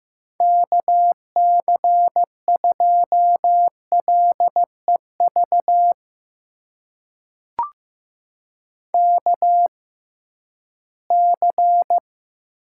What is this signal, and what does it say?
Morse code, heard as a keyed tone, about 15 wpm, spaced as for 11 wpm, 700 Hz. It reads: KC2LEV KC
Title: USA Call Signs - 6 Characters - 6x Character Spacing - 15wpm